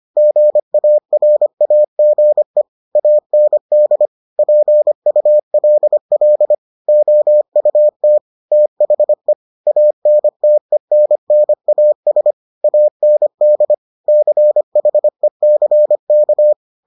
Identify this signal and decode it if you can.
Morse code, heard as a keyed tone, about 25 wpm, 600 Hz. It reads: GARAGE AND PULL OUT THE ANTENNAS AND CHECK